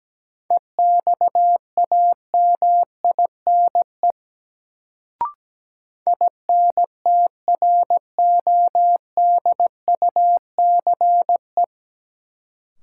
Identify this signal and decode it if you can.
Morse code, heard as a keyed tone, about 17 wpm, 700 Hz. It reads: EXAMINE INTRODUCE
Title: Sentences from Top 1600 Words - New Word Drill Alt - 17wpm